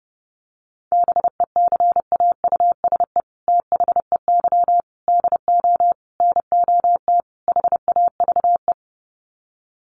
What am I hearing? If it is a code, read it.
Morse code, heard as a keyed tone, about 30 wpm, 700 Hz. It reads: BECAUSE THEY DO NOT HAVE
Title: Sets of 5 Words Encore2 30wpm